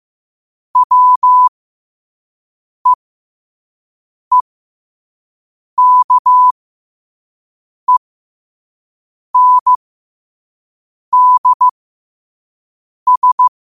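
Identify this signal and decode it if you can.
Morse code, heard as a keyed tone, about 15 wpm, spaced as for 5 wpm, 1000 Hz. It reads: WEEKENDS
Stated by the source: Morse quote of the day at 5 wpm.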